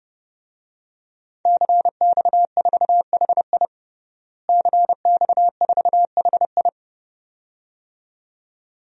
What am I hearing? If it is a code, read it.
Morse code, heard as a keyed tone, about 29 wpm, 700 Hz. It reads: CX4HI CX4HI
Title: DXCC-100 Call Signs - 5 Characters - Repeated 2x - 30wpm